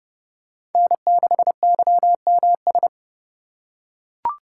Morse code, heard as a keyed tone, about 30 wpm, 700 Hz. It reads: N6YMS